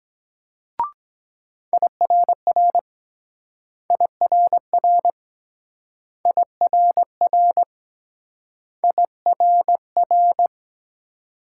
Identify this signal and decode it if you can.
Morse code, heard as a keyed tone, about 23 wpm, 700 Hz. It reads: IRR IRR IRR IRR